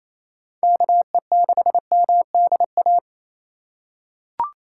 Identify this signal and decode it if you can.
Morse code, heard as a keyed tone, about 28 wpm, 700 Hz. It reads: KE6MDA
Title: USA Call Signs - 4x Character Spacing - 28wpm